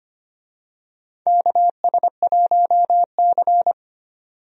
Morse code, heard as a keyed tone, about 25 wpm, 700 Hz. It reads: KS1C
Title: USA Call Signs - 4 Characters - 25wpm